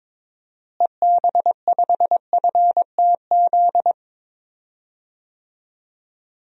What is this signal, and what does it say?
Morse code, heard as a keyed tone, about 22 wpm, 700 Hz. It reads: EB5FTZ
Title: DXCC-100 Call Signs - 6 Characters - Repeated 2x - 22wpm